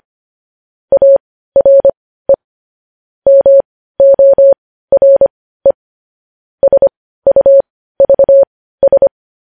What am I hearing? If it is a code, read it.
Morse code, heard as a keyed tone, about 25 wpm, spaced as for 15 wpm, 550 Hz. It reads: ARE MORE SUVS
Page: news headlines in morse code at 15 wpm